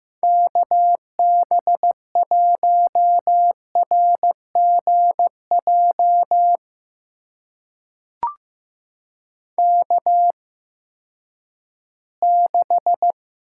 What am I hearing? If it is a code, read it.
Morse code, heard as a keyed tone, about 15 wpm, spaced as for 10 wpm, 700 Hz. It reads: KB1RGJ K6
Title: USA Call Signs - 8x Character Spacing - 15wpm